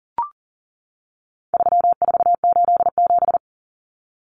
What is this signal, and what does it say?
Morse code, heard as a keyed tone, about 40 wpm, 700 Hz. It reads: 3487
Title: Numbers - 4-Digit 40wpm